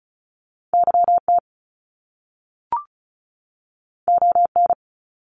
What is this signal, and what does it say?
Morse code, heard as a keyed tone, about 36 wpm, 700 Hz. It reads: YT ON
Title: Canadian Provinces and Territories 35wpm